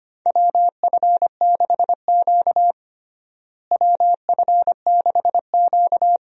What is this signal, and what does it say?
Morse code, heard as a keyed tone, about 25 wpm, 700 Hz. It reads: WF6Q WF6Q
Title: USA Call Signs - 4 Characters - Repeated 3x - 25wpm